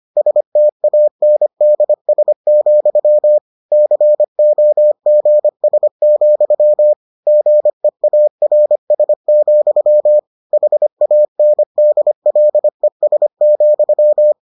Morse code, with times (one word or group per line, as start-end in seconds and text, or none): 0.2-3.4 STANDS,
3.7-6.9 COGS,
7.3-10.2 GEARS,
10.5-14.3 HANDLES,